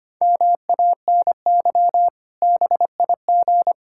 Morse code, heard as a keyed tone, about 25 wpm, 700 Hz. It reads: MANY BIG
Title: Pangram Challenge v2 - 25wpm